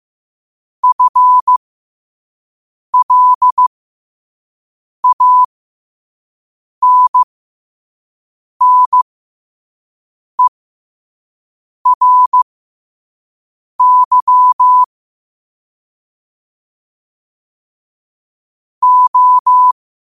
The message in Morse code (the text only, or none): FLANNERY O